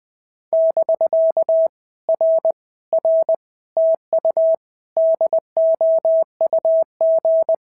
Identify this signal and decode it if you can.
Morse code, heard as a keyed tone, about 20 wpm, 650 Hz. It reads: <BK> R R TU DOUG
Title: POTA QSO Practice - 20wpm